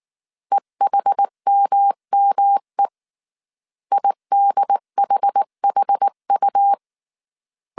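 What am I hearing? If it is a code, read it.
Morse code, heard as a keyed tone, about 19 wpm, 800 Hz. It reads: EHMME IDHHU
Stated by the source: Morse code dictation at 18 wpm from the IW3SGT QRS Morse code training course, five-character groups of 300 characters.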